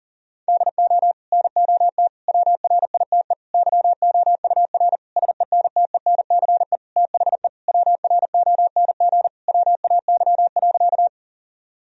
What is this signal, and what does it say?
Morse code, heard as a keyed tone, about 40 wpm, 700 Hz. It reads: DO NOT WRITE YOUR SENTENCE THE WRONG WAY.
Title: Sentences from Top 1000 Words - Review Part 1 - 40wpm